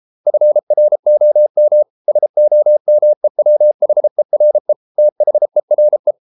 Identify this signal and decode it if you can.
Morse code, heard as a keyed tone, about 33 wpm, 600 Hz. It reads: FROM SOMEWHERE THERE